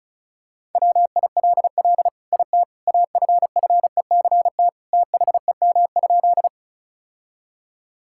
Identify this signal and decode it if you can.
Morse code, heard as a keyed tone, about 36 wpm, 700 Hz. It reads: WILL IT AFFECT THEM?
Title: Sentences from Top 1000 Words - Review Part 6 - 35wpm